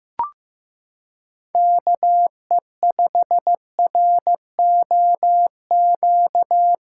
Morse code, heard as a keyed tone, about 16 wpm, 700 Hz. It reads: KE5ROQ
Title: USA Callsign DE Callsign - 6 Characters - 15wpm